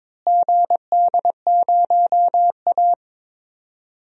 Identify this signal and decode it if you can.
Morse code, heard as a keyed tone, about 22 wpm, 700 Hz. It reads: GD0A
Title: DXCC-100 Call Signs - 4 Characters - Repeated 3x - 22wpm